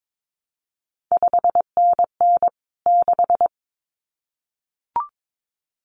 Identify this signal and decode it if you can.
Morse code, heard as a keyed tone, about 22 wpm, 700 Hz. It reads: 5NN 6
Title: CQ WW DX Contest Exchange Encore - 22wpm